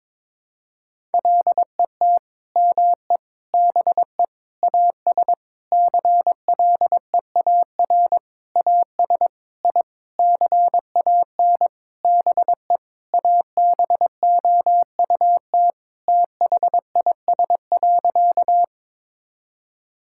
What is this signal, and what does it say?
Morse code, heard as a keyed tone, about 22 wpm, 700 Hz. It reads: LET ME BE AS CLEAR AS I CAN BE ABOUT THIS.
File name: Sentences with 9-Word Phrases - 22wpm